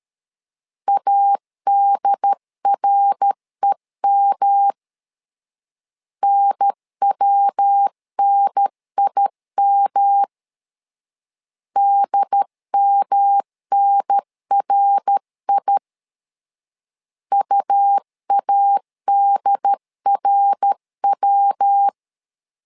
Morse code, heard as a keyed tone, about 13 wpm, 800 Hz. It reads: ADREM NWNIM DMNRI UADRW
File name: LEZIONI / 3 MWRO / 3 MWR 12wpm 1711